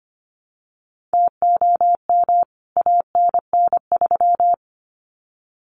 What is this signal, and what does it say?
Morse code, heard as a keyed tone, about 25 wpm, 700 Hz. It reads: TOM ANN3
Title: CWT Exchange 25wpm